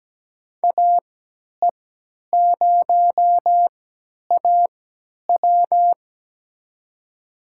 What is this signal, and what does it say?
Morse code, heard as a keyed tone, about 17 wpm, spaced as for 10 wpm, 700 Hz. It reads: AE0AW
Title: USA Call Signs - 5 Characters - 3x Character Spacing - 17wpm